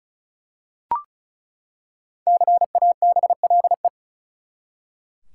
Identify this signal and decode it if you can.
Morse code, heard as a keyed tone, about 36 wpm, 700 Hz. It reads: CABLE
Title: Five Letter Words - 35wpm